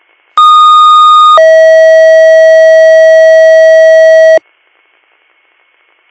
Two-tone page, heard as a one-second tone, then a three-second tone, about 1230 then 640 Hz. It is heard from 0.4 to 4.4 s.